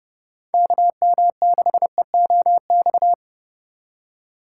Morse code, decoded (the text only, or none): KM6EOX